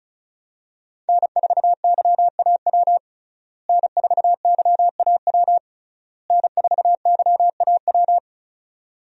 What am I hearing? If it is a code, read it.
Morse code, heard as a keyed tone, about 36 wpm, 700 Hz. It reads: N4YAW N4YAW N4YAW
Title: USA Call Signs - 5 Characters - Repeated 3x - 35wpm